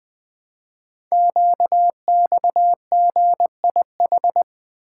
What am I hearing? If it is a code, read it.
Morse code, heard as a keyed tone, about 20 wpm, 700 Hz. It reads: QXGIH
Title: Code Groups - 5 Letters - 20wpm